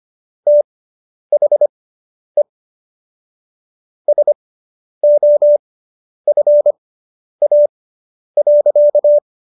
Morse code, heard as a keyed tone, about 25 wpm, spaced as for 10 wpm, 590 Hz. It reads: THE SOFA.